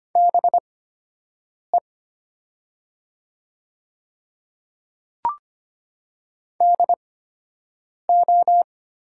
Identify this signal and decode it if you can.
Morse code, heard as a keyed tone, about 25 wpm, spaced as for 7 wpm, 700 Hz. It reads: BE DO
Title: 8x Character Spacing - Top 100 Words - 25wpm